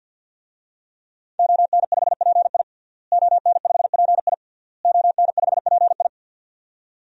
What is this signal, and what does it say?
Morse code, heard as a keyed tone, about 52 wpm, 700 Hz. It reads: ON5PI ON5PI ON5PI
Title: DXCC-100 Call Signs - 5 Characters - Repeated 3x - 50wpm